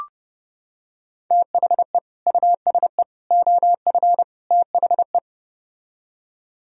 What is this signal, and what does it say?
Morse code, heard as a keyed tone, about 30 wpm, 700 Hz. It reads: THE USE OF THE